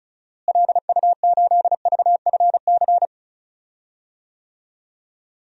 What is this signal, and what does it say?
Morse code, heard as a keyed tone, about 36 wpm, 700 Hz. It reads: LU8VFC
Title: DXCC-100 Call Signs - 6 Characters - Repeated 3x - 35wpm